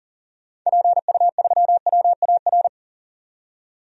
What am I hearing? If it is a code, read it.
Morse code, heard as a keyed tone, about 40 wpm, 700 Hz. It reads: PU3WAR
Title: DXCC-100 Call Signs - 6 Characters - 40wpm